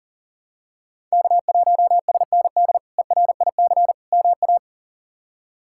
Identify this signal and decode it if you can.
Morse code, heard as a keyed tone, about 40 wpm, 700 Hz. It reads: K1SND ERIC MA
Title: K1USN SST QSO Practice - 40wpm